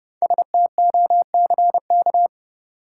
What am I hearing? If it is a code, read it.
Morse code, heard as a keyed tone, about 29 wpm, 700 Hz. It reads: STOCK
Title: Five Letter Words - 30wpm